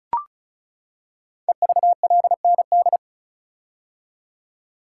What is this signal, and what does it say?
Morse code, heard as a keyed tone, about 36 wpm, 700 Hz. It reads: EVLND
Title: Code Groups - 5 Letter:Numbers - 35wpm